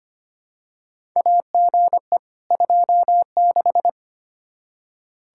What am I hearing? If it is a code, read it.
Morse code, heard as a keyed tone, about 25 wpm, 700 Hz. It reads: AGE 26
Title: Practice QSO - Exchange 3 - More Info - 25wpm